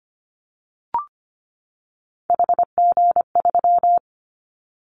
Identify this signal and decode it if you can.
Morse code, heard as a keyed tone, about 25 wpm, 700 Hz. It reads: HG3
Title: Callsign Prefixes for Top 50 DXCC - 25wpm